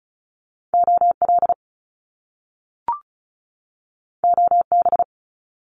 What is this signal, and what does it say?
Morse code, heard as a keyed tone, about 36 wpm, 700 Hz. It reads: OL OB
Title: Top Two Letter Combinations in Words 35wpm